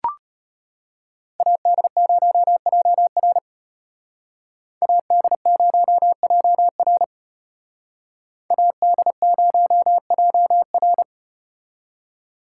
Morse code, heard as a keyed tone, about 34 wpm, 700 Hz. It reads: AD0JR AD0JR AD0JR